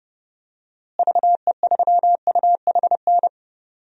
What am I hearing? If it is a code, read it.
Morse code, heard as a keyed tone, about 30 wpm, 700 Hz. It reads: VE3UHN
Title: DXCC-100 Call Signs - 6 Characters - Repeated 2x - 30wpm